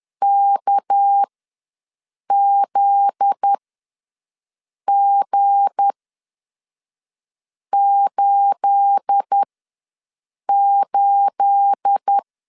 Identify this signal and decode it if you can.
Morse code, heard as a keyed tone, about 10 wpm, spaced as for 5 wpm, 800 Hz. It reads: KZG88
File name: LEZIONI / 9 28 ws 10wpm 4124